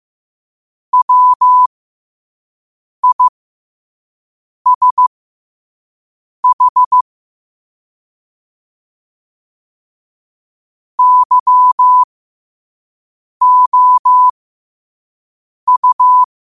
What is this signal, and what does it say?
Morse code, heard as a keyed tone, about 15 wpm, spaced as for 5 wpm, 1000 Hz. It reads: WISH YOU
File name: qotdfiles / 5wpm